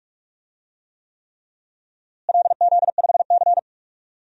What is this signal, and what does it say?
Morse code, heard as a keyed tone, about 43 wpm, 700 Hz. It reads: LZ5C
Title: DXCC-100 Call Signs - 4 Characters - Repeated 3x - 45wpm